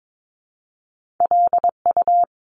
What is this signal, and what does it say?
Morse code, heard as a keyed tone, about 22 wpm, 700 Hz. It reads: LU
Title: Top Two Letter Combinations in Words 22wpm